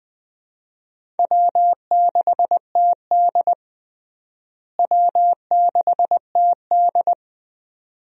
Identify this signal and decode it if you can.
Morse code, heard as a keyed tone, about 20 wpm, 700 Hz. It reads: W6TD W6TD
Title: USA Call Signs - 4 Characters - Repeated 2x - 20wpm